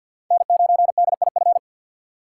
Morse code, heard as a keyed tone, about 47 wpm, 700 Hz. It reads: N9DIF
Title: USA Call Signs - 5 Characters - 2x Character Spacing - 50wpm